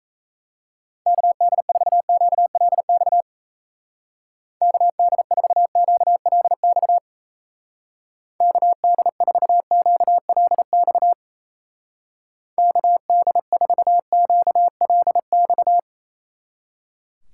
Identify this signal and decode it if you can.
Morse code, heard as a keyed tone, about 33 wpm, 700 Hz. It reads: KD4QLX KD4QLX KD4QLX KD4QLX
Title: Speed Racing - USA 2x3 Call Signs 28wpm